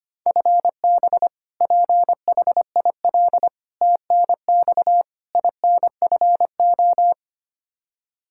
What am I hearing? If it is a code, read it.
Morse code, heard as a keyed tone, about 25 wpm, 700 Hz. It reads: FB PHIL TNX INFO
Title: Practice QSO - Exchange 3 - More Info - 25wpm